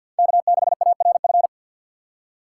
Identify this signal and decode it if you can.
Morse code, heard as a keyed tone, about 50 wpm, 700 Hz. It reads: K6ARF
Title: USA Call Signs - 6x Character Spacing - 50wpm